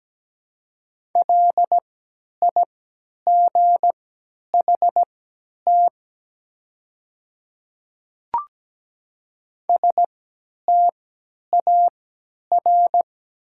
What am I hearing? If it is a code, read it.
Morse code, heard as a keyed tone, about 17 wpm, spaced as for 10 wpm, 700 Hz. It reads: LIGHT STAR
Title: 3x Character Spacing - Top 500 Words - 17wpm